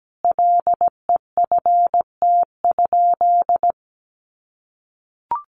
Morse code, heard as a keyed tone, about 17 wpm, 700 Hz. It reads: LEFT?
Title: Sentences from Top 1200 Words 17wpm